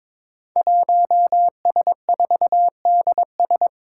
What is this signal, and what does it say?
Morse code, heard as a keyed tone, about 22 wpm, 700 Hz. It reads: 1S4DS
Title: Five Challenging Letter-Number Combinations 22wpm